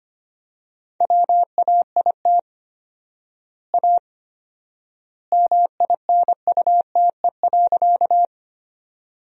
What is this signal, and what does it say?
Morse code, heard as a keyed tone, about 25 wpm, 700 Hz. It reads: WAIT A MINUTE.
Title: 4x Word-Spacing - Sentences from Top 500 Words - 25wpm